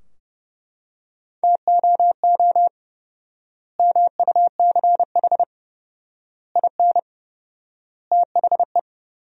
Morse code, heard as a keyed tone, about 32 wpm, 700 Hz. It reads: TOO MUCH IN THE
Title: 4x Word-Spacing - Sentences from Top 200 Words - 30wpm